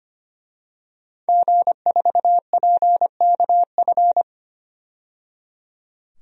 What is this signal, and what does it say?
Morse code, heard as a keyed tone, about 25 wpm, 700 Hz. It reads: G4PKF
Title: DXCC-100 Call Signs - 5 Characters - 25wpm